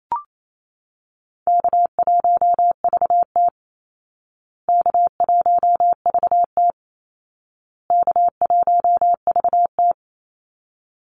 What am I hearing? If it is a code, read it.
Morse code, heard as a keyed tone, about 28 wpm, 700 Hz. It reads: K1VT K1VT K1VT
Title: Easy Call Signs Repeated 3x - 28wpm